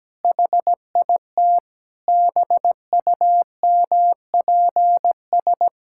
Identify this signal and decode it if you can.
Morse code, heard as a keyed tone, about 17 wpm, 700 Hz. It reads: HIT BUMPS
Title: Pangram Challenge v2 - 17wpm